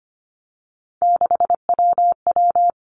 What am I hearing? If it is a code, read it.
Morse code, heard as a keyed tone, about 25 wpm, 700 Hz. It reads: 6WW
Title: Three Challenging Letter-Number Combinations 25wpm